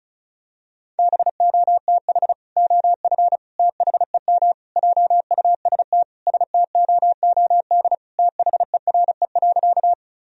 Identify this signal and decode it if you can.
Morse code, heard as a keyed tone, about 36 wpm, 700 Hz. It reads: BOTH OF THEM JUST STOOD THERE.
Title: Sentences from Top 1000 Words - Review Part 2 - 35wpm